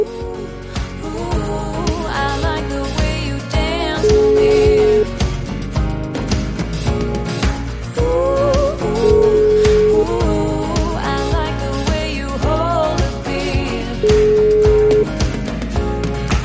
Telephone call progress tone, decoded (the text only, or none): ringback tone